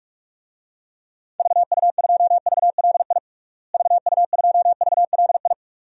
Morse code, heard as a keyed tone, about 47 wpm, 700 Hz. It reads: VU2VLI VU2VLI
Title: DXCC-100 Call Signs - 6 Characters - Repeated 2x - 45wpm